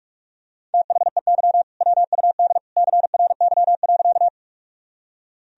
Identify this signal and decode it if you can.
Morse code, heard as a keyed tone, about 46 wpm, 700 Hz. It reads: THEY WUD CRY.
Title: Sentences from Top 500 Words and Abbreviations 45wpm